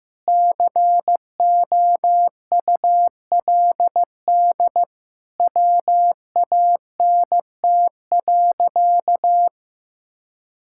Morse code, heard as a keyed tone, about 16 wpm, 700 Hz. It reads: COULD WANT.
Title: Sentences from Top 1000 Words - Review Part 5 - 15wpm